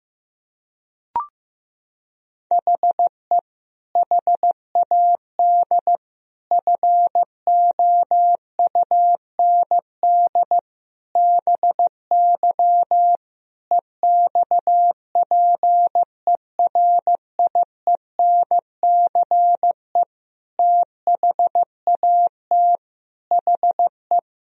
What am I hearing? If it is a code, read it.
Morse code, heard as a keyed tone, about 15 wpm, 700 Hz. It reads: HE HAD FOUND BY EXPERIENCE THAT HE
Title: Sentences from Top 1000 Words - Review Part 3 - 15wpm